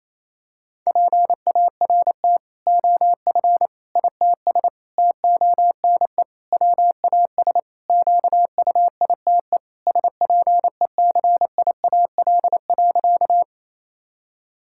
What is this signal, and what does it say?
Morse code, heard as a keyed tone, about 28 wpm, 700 Hz. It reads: PART OF ITS TONE WAS QUITE SPECIAL.
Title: Sentences from Top 1600 Words 28wpm